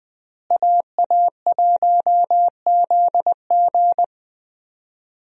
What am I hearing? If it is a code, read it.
Morse code, heard as a keyed tone, about 20 wpm, 700 Hz. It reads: AA1ZG
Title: USA Call Signs - 5 Characters - Repeated 3x - 20wpm